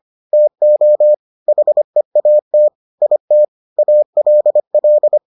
Morse code, heard as a keyed tone, about 25 wpm, 600 Hz. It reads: TO HEAT IT ALL